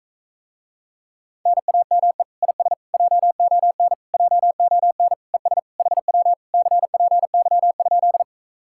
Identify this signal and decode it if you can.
Morse code, heard as a keyed tone, about 44 wpm, 700 Hz. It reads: NAME IS JON JON ES HW CPY?